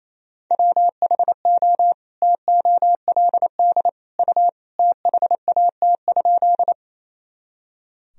Morse code, heard as a keyed tone, about 28 wpm, 700 Hz. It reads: WHO TOLD U THAT?
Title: Sentences from Top 400 Words and Abbreviations 28wpm